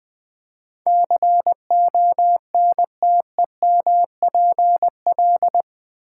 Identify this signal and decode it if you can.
Morse code, heard as a keyed tone, about 20 wpm, 700 Hz. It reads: CONTEMPL
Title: Build A Word - 4 Letters At A Time - 11-Letter Words - 20wpm